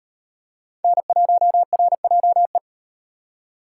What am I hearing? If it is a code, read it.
Morse code, heard as a keyed tone, about 38 wpm, 700 Hz. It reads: N1RJE